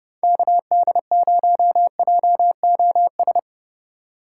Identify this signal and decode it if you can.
Morse code, heard as a keyed tone, about 30 wpm, 700 Hz. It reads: KD0JOS
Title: USA Call Signs - 2x Character Spacing - 30wpm